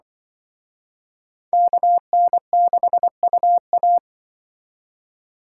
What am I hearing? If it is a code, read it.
Morse code, heard as a keyed tone, about 25 wpm, 700 Hz. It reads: KN6UA
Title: Speed Racing - USA Call Signs 20wpm